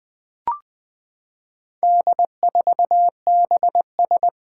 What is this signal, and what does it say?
Morse code, heard as a keyed tone, about 20 wpm, 700 Hz. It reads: D4BS
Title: Four Challenging Letter-Number Combinations 20wpm